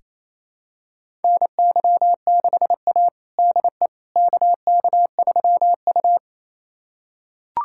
Morse code, heard as a keyed tone, about 28 wpm, 700 Hz. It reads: NY6A DE KK3U
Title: USA Callsign DE Callsign - 4 Characters - 28wpm